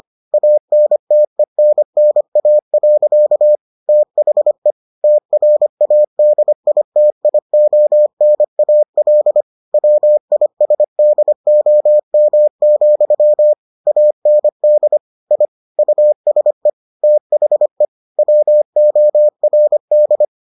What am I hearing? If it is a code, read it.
Morse code, heard as a keyed tone, about 25 wpm, 600 Hz. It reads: ANTENNA. THE TRADITIONAL WISDOM, AND I USE THE WORD